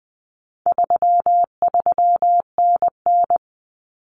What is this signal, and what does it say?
Morse code, heard as a keyed tone, about 20 wpm, 700 Hz. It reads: 33NN
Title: Common Cut Numbers - 4-Digit 20wpm